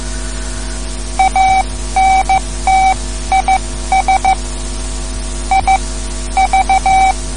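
Morse code, heard as a keyed tone, about 14 wpm, spaced as for 12 wpm, 770 Hz. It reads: ANT IS IV
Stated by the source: Morse code lesson, third lesson in a series for learning the Morse alphabet.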